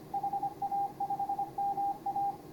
Morse code, heard as a keyed tone, about 26 wpm, 810 Hz, with radio noise. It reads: HA5MA
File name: CW / HA5MA RHOB HANG CW 2017